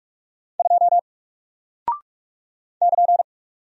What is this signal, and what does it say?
Morse code, heard as a keyed tone, about 44 wpm, 700 Hz. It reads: 2 <KN>
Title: Single Character (Letter:Number:Punctuation:Prosign) 45wpm